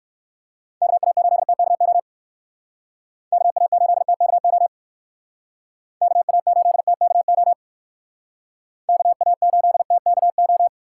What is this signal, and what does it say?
Morse code, heard as a keyed tone, about 58 wpm, 700 Hz. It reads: KA8TKO KA8TKO KA8TKO KA8TKO